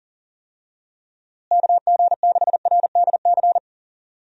Morse code, heard as a keyed tone, about 40 wpm, 700 Hz. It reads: KG6RDC
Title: USA Call Signs - 6 Characters - Repeated 2x - 40wpm